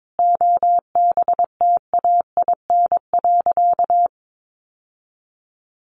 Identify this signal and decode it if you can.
Morse code, heard as a keyed tone, about 22 wpm, 700 Hz. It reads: OBTAIN.
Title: Sentences from Top 1300 Words 22wpm